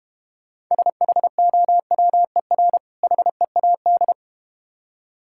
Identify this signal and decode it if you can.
Morse code, heard as a keyed tone, about 31 wpm, 700 Hz. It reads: SHOWER HEAD
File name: Easy Words - Speed-Racing with Hints - 25wpm